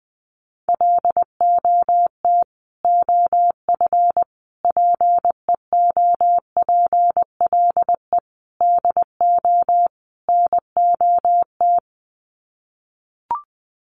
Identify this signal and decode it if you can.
Morse code, heard as a keyed tone, about 19 wpm, 700 Hz. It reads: LOT OF PEOPLE DO NOT
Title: Sets of 5 Words Encore 20wpm